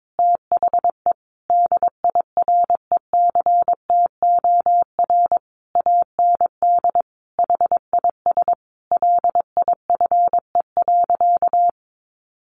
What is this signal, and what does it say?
Morse code, heard as a keyed tone, about 22 wpm, 700 Hz. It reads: THE DIRECTOR AND HIS LIFE.
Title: Sentences from Top 1300 Words 22wpm